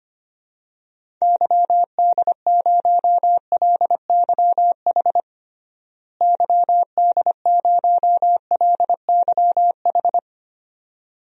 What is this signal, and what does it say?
Morse code, heard as a keyed tone, about 25 wpm, 700 Hz. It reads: YD0LYH YD0LYH
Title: DXCC-100 Call Signs - 6 Characters - Repeated 2x - 25wpm